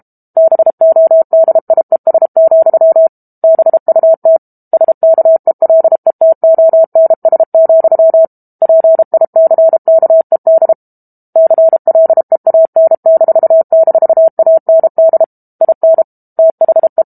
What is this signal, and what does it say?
Morse code, heard as a keyed tone, about 32 wpm, 650 Hz. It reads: BODIES, BUT SKELETONS, PICKED CLEAN--AND IN THE
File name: Book Two / 06-Chapter 6 - 32 WPM